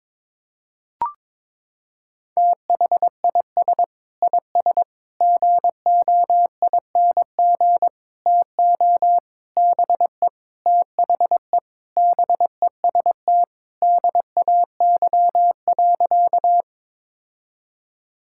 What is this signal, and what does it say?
Morse code, heard as a keyed tone, about 22 wpm, 700 Hz. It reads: THIS IS GOING TO BE THE BEST DAY.
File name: practice / 22wpm / Sentences with 4-Word Phrases - 22wpm